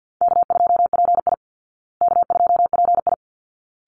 Morse code, heard as a keyed tone, about 50 wpm, 700 Hz. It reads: K2PI K2PI
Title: Easy Call Signs Repeated 2x - 50wpm